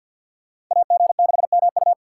Morse code, heard as a keyed tone, about 53 wpm, 700 Hz. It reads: AG6MU
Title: USA Call Signs - 5 Characters - 6x Character Spacing - 50wpm